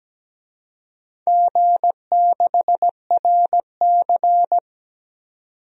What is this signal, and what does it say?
Morse code, heard as a keyed tone, about 17 wpm, 700 Hz. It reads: G6RC
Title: DXCC-100 Call Signs - 4 Characters - 17wpm